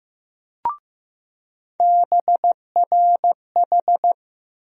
Morse code, heard as a keyed tone, about 15 wpm, 700 Hz. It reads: BRH
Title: USA Call Sign Postfixes - 3 Letters 15wpm